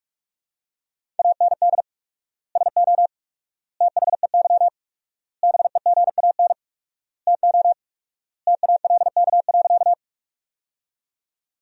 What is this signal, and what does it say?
Morse code, heard as a keyed tone, about 42 wpm, 700 Hz. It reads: AND SO THEY BEGAN TO TALK.
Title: 4x Word-Spacing - Sentences from Top 400 Words - 45wpm